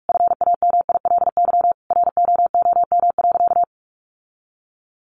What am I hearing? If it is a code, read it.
Morse code, heard as a keyed tone, about 44 wpm, 700 Hz. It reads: FAMILY ROOM.
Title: Sentences from Top 1100 Words 45wpm